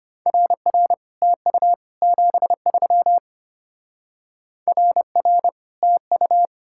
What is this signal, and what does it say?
Morse code, heard as a keyed tone, about 31 wpm, 700 Hz. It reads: RR TU 73 RR TU
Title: Speed-Racing Common QSO Phrases - 25wpm